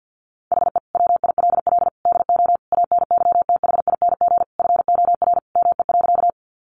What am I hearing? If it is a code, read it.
Morse code, heard as a keyed tone, about 50 wpm, 700 Hz. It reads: HE WILL DO ANYTHING FOR ME.
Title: Sentences from Top 1000 Words 50wpm